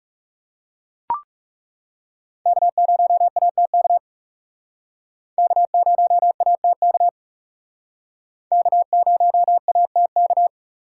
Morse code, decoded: K0ATK K0ATK K0ATK

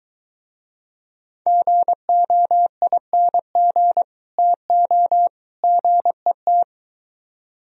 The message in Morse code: GOING TO GET

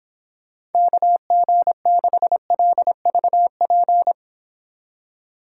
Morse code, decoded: KG6LVP

